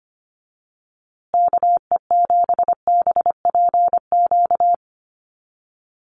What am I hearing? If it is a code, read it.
Morse code, heard as a keyed tone, about 25 wpm, 700 Hz. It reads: KE7BPQ
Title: USA Call Signs Repeated 2x 25wpm